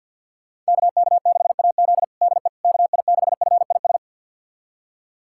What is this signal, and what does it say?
Morse code, heard as a keyed tone, about 51 wpm, 700 Hz. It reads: KK6AZ DE KI6FIS
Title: USA Callsign DE Callsign 50wpm